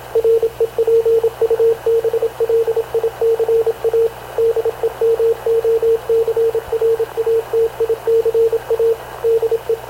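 Morse code, heard as a keyed tone, about 26 wpm, 460 Hz, with radio noise. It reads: REPUBLICA DEMOCRATICA DE
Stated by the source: Havana Radio / CLA sending Spanish press (PX) in Morse, recorded by Radio Officer Paul Dumesnil onboard CNAV QUEST/CZDO.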